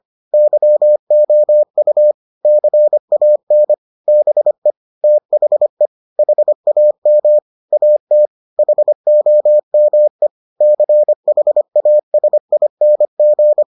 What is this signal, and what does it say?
Morse code, heard as a keyed tone, about 25 wpm, 600 Hz. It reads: YOU CAN BE THE HAM AT HOME CHASING